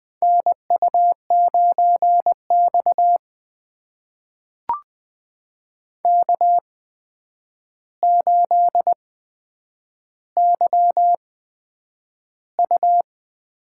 Morse code, heard as a keyed tone, about 21 wpm, spaced as for 9 wpm, 700 Hz. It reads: NU9X K8YU
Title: USA Call Signs - 4 Characters - 8x Character Spacing - 20wpm